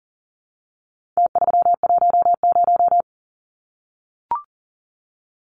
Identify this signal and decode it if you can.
Morse code, heard as a keyed tone, about 40 wpm, 700 Hz. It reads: T310
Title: Numbers - Contest Style - 4-Digit 40wpm